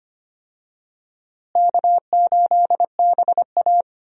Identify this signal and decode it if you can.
Morse code, heard as a keyed tone, about 25 wpm, 700 Hz. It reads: K8BA
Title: USA Call Signs - 4 Characters - Repeated 3x - 25wpm